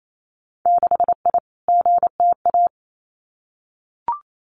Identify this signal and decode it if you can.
Morse code, heard as a keyed tone, about 28 wpm, 700 Hz. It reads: 6I GTA
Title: Winter Field Day Exchanges Encore 28wpm